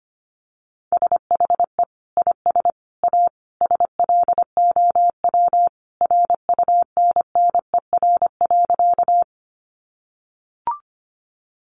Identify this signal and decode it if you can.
Morse code, heard as a keyed tone, about 25 wpm, 700 Hz. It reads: SHE IS A SLOW RUNNER.
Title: Additional Easy Phrases 25wpm